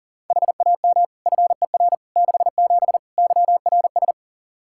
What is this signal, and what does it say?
Morse code, heard as a keyed tone, about 40 wpm, 700 Hz. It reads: HAM FER 67 YRS
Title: Practice QSO - Exchange 3 - More Info - 40wpm